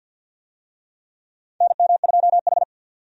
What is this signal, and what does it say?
Morse code, heard as a keyed tone, about 48 wpm, 700 Hz. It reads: NM2H
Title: USA Call Signs - 4 Characters - 50wpm